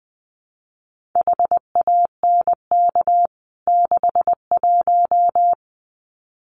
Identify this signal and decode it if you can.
Morse code, heard as a keyed tone, about 20 wpm, 700 Hz. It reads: HANK 61
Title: CWT Exchange 20wpm